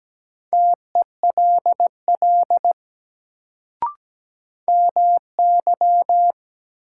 Morse code, heard as a keyed tone, about 17 wpm, 700 Hz. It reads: TELL MY
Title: Sentences from Top 200 Words - New Word Drill - 17wpm